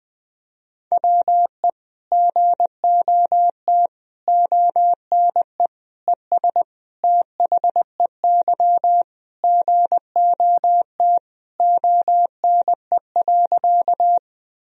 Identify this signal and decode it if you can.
Morse code, heard as a keyed tone, about 20 wpm, 700 Hz. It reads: WE GOT ONE ES THEY GOT ONE.
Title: Sentences from Top 300 Words and Abbreviations 20wpm